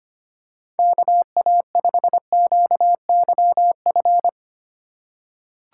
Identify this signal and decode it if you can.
Morse code, heard as a keyed tone, about 25 wpm, 700 Hz. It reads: KA5QYF
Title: USA Call Signs - 6 Characters - 3x Character Spacing - 25wpm